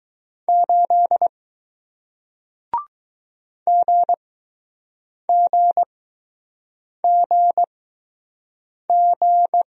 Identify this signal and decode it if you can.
Morse code, heard as a keyed tone, about 20 wpm, spaced as for 6 wpm, 700 Hz. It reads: 8 GGGG